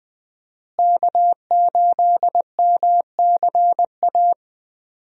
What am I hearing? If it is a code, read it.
Morse code, heard as a keyed tone, about 20 wpm, 700 Hz. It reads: K8MCA
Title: USA Call Signs - 5 Characters - Repeated 2x - 20wpm